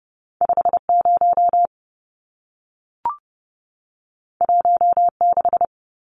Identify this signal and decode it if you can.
Morse code, heard as a keyed tone, about 30 wpm, 700 Hz. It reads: 50 16
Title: Numbers - 2-Digit 30wpm